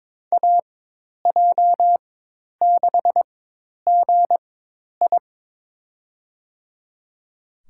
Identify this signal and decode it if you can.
Morse code, heard as a keyed tone, about 22 wpm, spaced as for 10 wpm, 700 Hz. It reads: AJ6GI
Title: USA Call Signs - 5 Characters - 4x Character Spacing - 22wpm